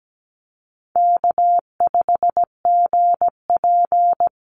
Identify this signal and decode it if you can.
Morse code, heard as a keyed tone, about 17 wpm, 700 Hz. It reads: K5GP
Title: Call Signs Repeated 2x - 17wpm